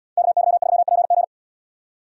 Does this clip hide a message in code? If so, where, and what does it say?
Yes: Morse code from 0.2 to 1.2 s, heard as a keyed tone, about 76 wpm, 700 Hz. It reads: KY3PG